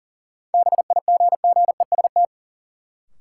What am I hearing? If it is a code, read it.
Morse code, heard as a keyed tone, about 40 wpm, 700 Hz. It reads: BIGGEST
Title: Sets of 4 Words - New Word Drill - 40wpm